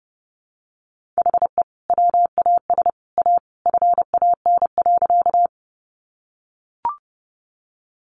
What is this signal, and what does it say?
Morse code, heard as a keyed tone, about 30 wpm, 700 Hz. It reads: HE WAS A FAN.
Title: Sentences from Top 1300 Words 30wpm